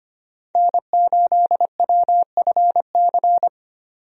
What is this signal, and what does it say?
Morse code, heard as a keyed tone, about 25 wpm, 690 Hz. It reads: N8WFC
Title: USA Call Signs - 5 Characters - 25wpm